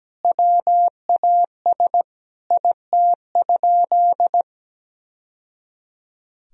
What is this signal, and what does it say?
Morse code, heard as a keyed tone, about 17 wpm, 700 Hz. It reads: WAS IT?
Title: Sentences from Top 1000 Words - Review Part 4 - 17wpm